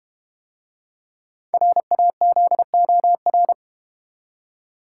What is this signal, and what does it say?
Morse code, heard as a keyed tone, about 32 wpm, 700 Hz. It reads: RAZOR